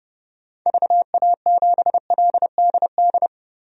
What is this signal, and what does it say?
Morse code, heard as a keyed tone, about 30 wpm, 700 Hz. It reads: VA7LDD
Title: DXCC-100 Call Signs - 6 Characters - 30wpm